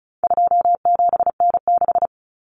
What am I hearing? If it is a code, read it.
Morse code, heard as a keyed tone, about 34 wpm, 700 Hz. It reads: 27N6
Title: Common Cut Numbers - 4-Digit 35wpm